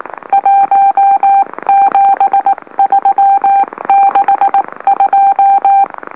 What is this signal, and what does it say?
Morse code, heard as a keyed tone, about 18 wpm, 790 Hz. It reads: 17362
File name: M04- -2345utc-LOLO